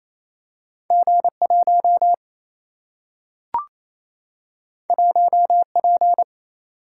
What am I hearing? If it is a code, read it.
Morse code, heard as a keyed tone, about 28 wpm, 700 Hz. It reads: G 1 1 P
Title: Single-Letter Single-Number 28wpm